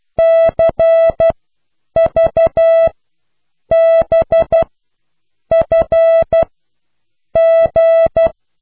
Morse code, heard as a keyed tone, about 12 wpm, spaced as for 7 wpm, 660 Hz. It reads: CVBFG
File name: Alles lage snelheid